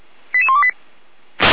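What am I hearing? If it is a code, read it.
Five-tone selective call (ZVEI), heard as quick runs of five tones, tones of about 70 ms. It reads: 88128
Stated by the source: ZVEI five-tone selective call received on SDR.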